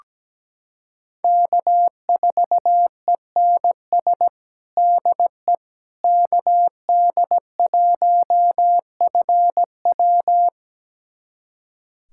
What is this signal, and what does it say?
Morse code, heard as a keyed tone, about 17 wpm, 700 Hz. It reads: K4ENS DE KD1FW
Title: USA Callsign DE Callsign 17wpm